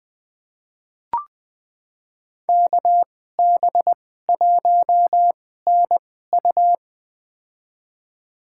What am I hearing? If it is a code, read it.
Morse code, heard as a keyed tone, about 21 wpm, spaced as for 15 wpm, 700 Hz. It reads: KB1NU
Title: USA Call Signs - 2x Character Spacing - 20wpm